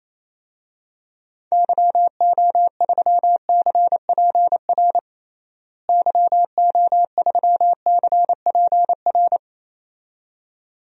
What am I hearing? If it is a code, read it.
Morse code, heard as a keyed tone, about 28 wpm, 700 Hz. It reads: YO3CPR YO3CPR
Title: DXCC-100 Call Signs - 6 Characters - Repeated 2x - 28wpm